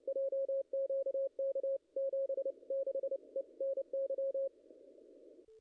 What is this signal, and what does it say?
Morse code, heard as a keyed tone, about 29 wpm, 540 Hz. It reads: JQK 76 ENY
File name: contests / ss-cw-k5tr-op-n5rz -0000